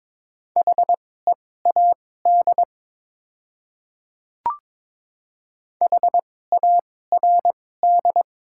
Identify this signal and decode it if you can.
Morse code, heard as a keyed tone, about 22 wpm, spaced as for 16 wpm, 700 Hz. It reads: HEAD HARD
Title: 2x Character Spacing - Top 300 Words - 22wpm